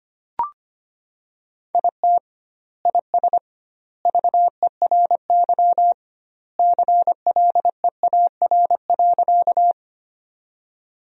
Morse code, heard as a keyed tone, about 25 wpm, 700 Hz. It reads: IT IS VERY CLEAR.
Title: 2x Word-Spacing - Sentences from Top 500 Words - 25wpm